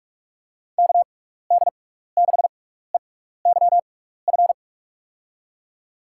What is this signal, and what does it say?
Morse code, heard as a keyed tone, about 46 wpm, spaced as for 16 wpm, 700 Hz. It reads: KD6EYF
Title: USA Call Signs - 6x Character Spacing - 45wpm